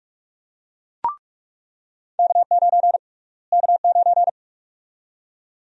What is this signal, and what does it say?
Morse code, heard as a keyed tone, about 46 wpm, 700 Hz. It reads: K9 K9
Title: USA Call Sign Prefixes - Repeated 2x - 45wpm